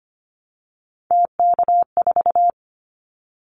Morse code, heard as a keyed tone, about 25 wpm, 700 Hz. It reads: TK4
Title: Callsign Prefixes for Top 100 DXCC - 25wpm